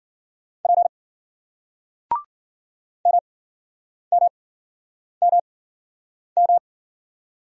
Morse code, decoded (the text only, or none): P MMMM